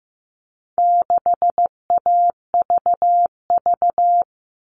Morse code, heard as a keyed tone, about 15 wpm, 700 Hz. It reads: 6AVV
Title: All Cut Numbers - 4-Digit 15wpm